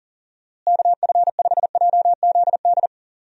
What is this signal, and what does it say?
Morse code, heard as a keyed tone, about 40 wpm, 700 Hz. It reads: KF5JZD
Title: USA Call Signs - 6 Characters - Repeated 2x - 40wpm